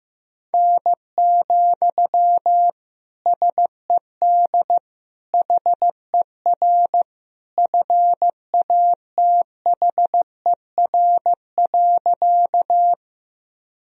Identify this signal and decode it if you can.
Morse code, heard as a keyed tone, about 15 wpm, 700 Hz. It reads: N, SED HER FATHER.
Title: Sentences from Top 300 Words and Abbreviations 15wpm